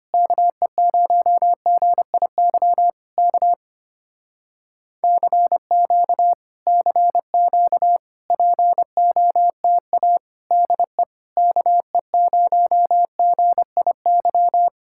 Morse code, heard as a keyed tone, about 24 wpm, 700 Hz. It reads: KE0GIY K CQ CQ POTA DE KE0GIY
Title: Speed-Racing Common QSO Phrases - 25wpm